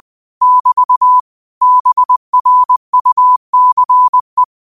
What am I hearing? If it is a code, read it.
Morse code, heard as a keyed tone, about 20 wpm, 1000 Hz. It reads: <BT> BRUCE
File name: qotdfiles / 20wpm